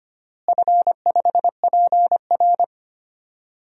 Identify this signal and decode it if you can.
Morse code, heard as a keyed tone, about 25 wpm, 700 Hz. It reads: F5PR
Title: DXCC-100 Call Signs - 4 Characters - Repeated 3x - 25wpm